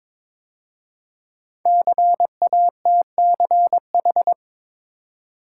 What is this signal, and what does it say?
Morse code, heard as a keyed tone, about 22 wpm, 700 Hz. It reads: CATCH
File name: 5 Letter Words - First 3 Letters Given - 22wpm